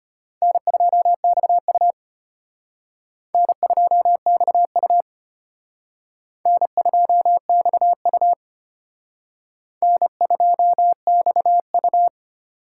Morse code, heard as a keyed tone, about 31 wpm, 700 Hz. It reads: N2XU N2XU N2XU N2XU
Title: Speed Racing - USA 1x2 Call Signs 25wpm